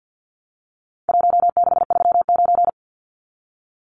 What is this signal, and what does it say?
Morse code, heard as a keyed tone, about 50 wpm, 700 Hz. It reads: 1639
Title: Numbers - Contest Style - 4-Digit 50wpm